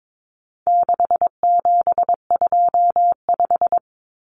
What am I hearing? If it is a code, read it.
Morse code, heard as a keyed tone, about 22 wpm, 700 Hz. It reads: 6725
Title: Numbers - 4-Digit 22wpm